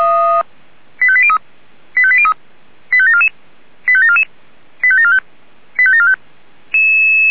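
Five-tone selective call (ZVEI), heard as quick runs of five tones, tones of about 70 ms. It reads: 85803 85803 85744 85744 85745 85745
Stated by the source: ZVEI five-tone selective call received on SDR.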